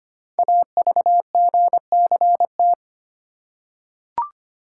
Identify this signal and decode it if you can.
Morse code, heard as a keyed tone, about 25 wpm, 700 Hz. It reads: AVGCT
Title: Code Groups - 5 Letter:Numbers - 25wpm